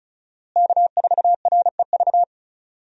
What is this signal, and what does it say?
Morse code, heard as a keyed tone, about 36 wpm, 700 Hz. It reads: K4REV
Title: USA Call Signs - 5 Characters - Repeated 2x - 35wpm